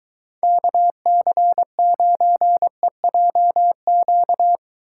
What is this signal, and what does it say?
Morse code, heard as a keyed tone, about 23 wpm, 700 Hz. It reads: KC9EJQ